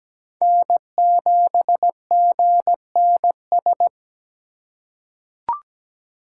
Morse code, heard as a keyed tone, about 17 wpm, 700 Hz. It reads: N7GNS
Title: USA Call Signs - 5 Characters - 4x Character Spacing - 17wpm